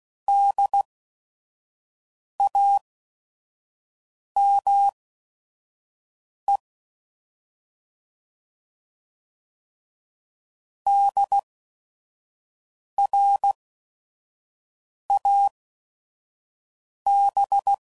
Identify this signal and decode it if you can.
Morse code, heard as a keyed tone, about 16 wpm, spaced as for 5 wpm, 800 Hz. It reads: DAME DRAB
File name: Prefixes / By First Letter / Ds